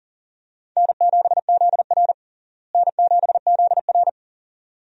Morse code, heard as a keyed tone, about 40 wpm, 700 Hz. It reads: N7ZR N7ZR
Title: USA Call Signs - 4 Characters - Repeated 2x - 40wpm